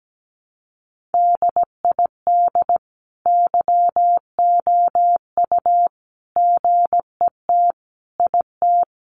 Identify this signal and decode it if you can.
Morse code, heard as a keyed tone, about 17 wpm, 700 Hz. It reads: DID YOU GET IT
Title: Sentences from Top 1300 Words 17wpm